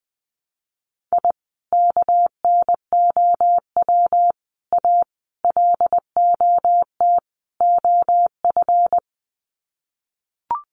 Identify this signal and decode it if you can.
Morse code, heard as a keyed tone, about 20 wpm, 700 Hz. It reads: I KNOW A LOT OF
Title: Sets of 5 Words Encore 20wpm